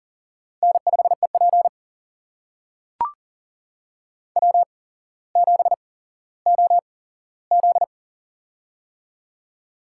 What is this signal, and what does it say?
Morse code, heard as a keyed tone, about 42 wpm, spaced as for 17 wpm, 700 Hz. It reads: N5EP W7OZ
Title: USA Call Signs - 4 Characters - 8x Character Spacing - 40wpm